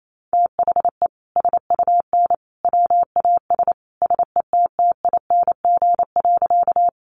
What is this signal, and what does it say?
Morse code, heard as a keyed tone, about 28 wpm, 700 Hz. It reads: THE SUN WAS SETTING.
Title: Sentences from Top 1300 Words 28wpm